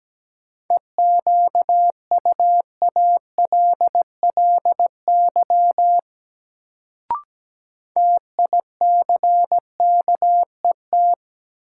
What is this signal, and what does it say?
Morse code, heard as a keyed tone, about 17 wpm, 700 Hz. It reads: EQUALLY TICKET
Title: Sentences from Top 1400 Words - New Word Drill Alt - 17wpm